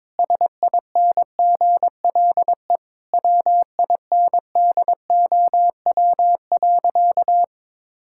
Morse code, heard as a keyed tone, about 22 wpm, 700 Hz. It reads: SINGLE WINDOW.
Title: Sentences from Top 1600 Words 22wpm